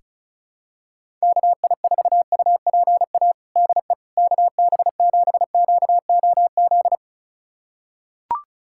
Morse code, heard as a keyed tone, about 36 wpm, 700 Hz. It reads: KI4UPA DE KB7QOZ
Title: USA Callsign DE Callsign 35wpm